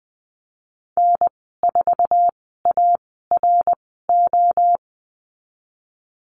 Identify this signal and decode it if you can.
Morse code, heard as a keyed tone, about 20 wpm, 700 Hz. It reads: N 4 A R O
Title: 062 Lesson 11g - Letter h - Call Signs 20wpm